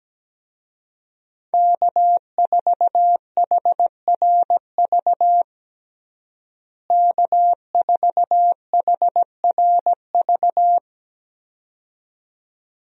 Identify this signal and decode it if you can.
Morse code, heard as a keyed tone, about 17 wpm, 700 Hz. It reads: K4HRV K4HRV
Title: USA Call Signs - 5 Characters - Repeated 2x - 17wpm